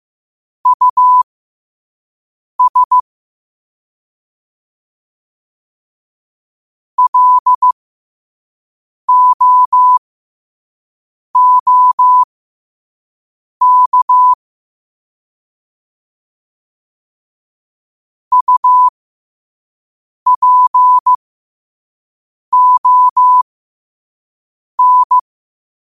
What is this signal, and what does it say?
Morse code, heard as a keyed tone, about 15 wpm, spaced as for 5 wpm, 1000 Hz. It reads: US LOOK UPON
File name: qotdfiles / 5wpm